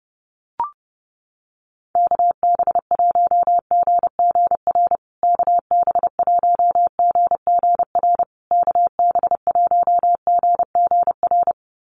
Morse code, heard as a keyed tone, about 30 wpm, 700 Hz. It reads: KB1GGR KB1GGR KB1GGR
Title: USA Call Signs Repeated 3x 30wpm